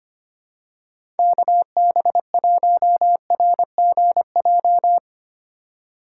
Morse code, heard as a keyed tone, about 25 wpm, 690 Hz. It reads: KB1RGJ
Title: USA Call Signs - 6 Characters - 3x Character Spacing - 25wpm